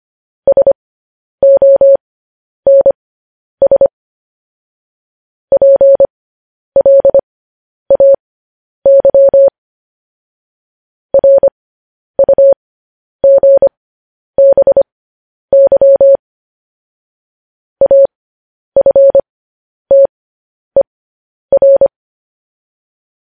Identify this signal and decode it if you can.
Morse code, heard as a keyed tone, about 25 wpm, spaced as for 10 wpm, 550 Hz. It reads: SONS PLAY RUGBY AFTER
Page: news headlines in morse code at 10 wpm